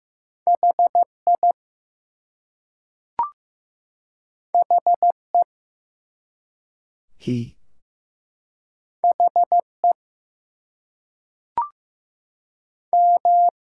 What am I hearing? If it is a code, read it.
Morse code, heard as a keyed tone, about 15 wpm, 700 Hz. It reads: HI HE HE M
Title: Two Letter Words - 15wpm